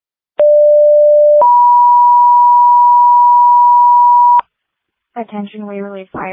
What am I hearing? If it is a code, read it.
Two-tone page, heard as a one-second tone, then a three-second tone, about 590 then 970 Hz, with speech. It is heard from 0.4 to 4.4 s.